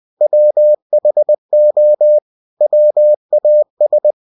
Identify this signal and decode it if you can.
Morse code, heard as a keyed tone, about 20 wpm, 600 Hz. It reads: WHO WAS
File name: winnie-20wpm